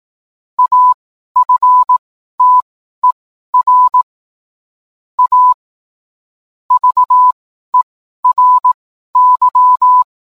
Morse code, heard as a keyed tone, about 18 wpm, spaced as for 12 wpm, 1000 Hz. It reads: AFTER A VERY